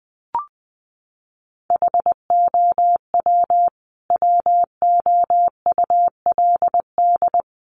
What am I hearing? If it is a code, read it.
Morse code, heard as a keyed tone, about 20 wpm, 700 Hz. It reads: HOW WOULD
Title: Sentences from Top 100 Words 20wpm